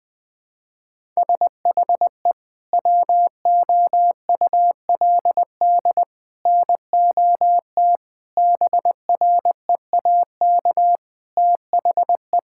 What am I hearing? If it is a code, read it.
Morse code, heard as a keyed tone, about 20 wpm, 700 Hz. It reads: SHE WOULD NOT BREAK THE
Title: Sentences from Top 1000 Words - Review Part 6 - 20wpm